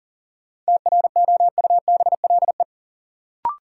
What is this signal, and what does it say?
Morse code, heard as a keyed tone, about 40 wpm, 700 Hz. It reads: TROUBLE
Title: Sentences from Top 1200 Words - New Word Drill - 40wpm